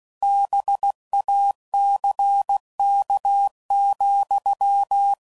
Morse code, heard as a keyed tone, about 16 wpm, 800 Hz. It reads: BACK,
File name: CW Code Groups @15WPM / QSOs / kc7haa